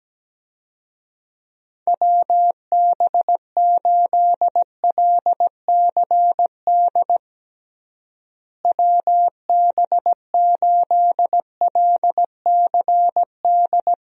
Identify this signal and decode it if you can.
Morse code, heard as a keyed tone, about 17 wpm, 700 Hz. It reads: WB8LCD WB8LCD
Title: USA Call Signs - 6 Characters - Repeated 2x - 17wpm